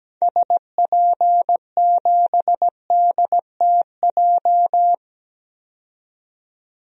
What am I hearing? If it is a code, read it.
Morse code, heard as a keyed tone, about 17 wpm, 700 Hz. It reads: SP7DTJ
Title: DXCC-100 Call Signs - 6 Characters - 17wpm